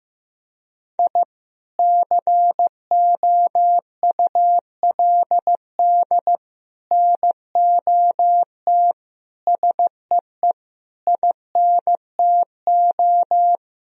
Morse code, heard as a keyed tone, about 15 wpm, 700 Hz. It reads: I COULD NOT SEE INTO
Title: Sentences from Top 1000 Words - Review Part 4 - 15wpm